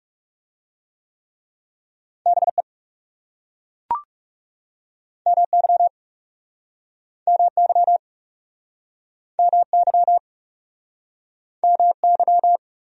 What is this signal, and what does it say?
Morse code, heard as a keyed tone, about 40 wpm, 700 Hz. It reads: BE MY MY MY MY